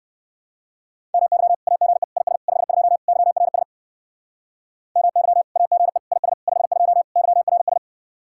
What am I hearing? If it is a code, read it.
Morse code, heard as a keyed tone, about 59 wpm, 700 Hz. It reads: MY AGE IS 52 YRS MY AGE IS 52 YRS